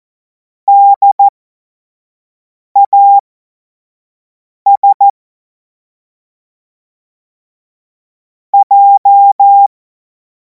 Morse code, heard as a keyed tone, about 14 wpm, spaced as for 5 wpm, 800 Hz. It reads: DAS J
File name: Morsekurs / Witze / Witz 02. Juni 2026 25 F70